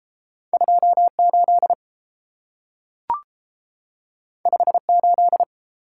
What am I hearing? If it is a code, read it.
Morse code, heard as a keyed tone, about 30 wpm, 700 Hz. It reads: 28 58